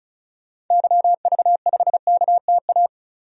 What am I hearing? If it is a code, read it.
Morse code, heard as a keyed tone, about 36 wpm, 690 Hz. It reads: YV5KTA
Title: DXCC-100 Call Signs - 6 Characters - 35wpm